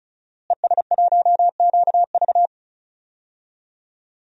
Morse code, heard as a keyed tone, about 34 wpm, 700 Hz. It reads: ES1QV
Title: DXCC-100 Call Signs - 5 Characters - Repeated 3x - 35wpm